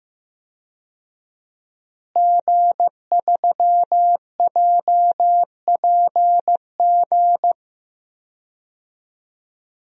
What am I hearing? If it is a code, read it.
Morse code, heard as a keyed tone, about 15 wpm, 700 Hz. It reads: G3JPG
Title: DXCC-100 Call Signs - 5 Characters - Repeated 2x - 15wpm